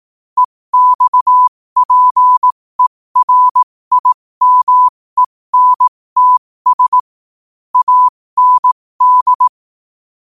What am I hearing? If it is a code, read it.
Morse code, heard as a keyed tone, about 18 wpm, spaced as for 15 wpm, 1000 Hz. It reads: EXPERIMENTS AND
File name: qotdfiles / 15wpm